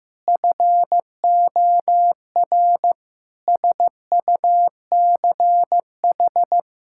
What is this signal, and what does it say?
Morse code, heard as a keyed tone, about 15 wpm, 700 Hz. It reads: FOR SUCH
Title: Sentences from Top 1000 Words - Review Part 6 - 15wpm